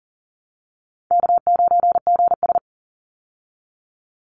Morse code, heard as a keyed tone, about 40 wpm, 700 Hz. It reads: K9GS
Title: Easy Call Signs Repeated 3x - 40wpm